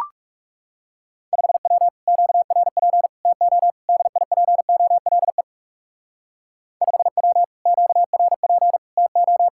Morse code, heard as a keyed tone, about 38 wpm, 700 Hz. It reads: 5W QRP TO DIPOLE 5W QRP TO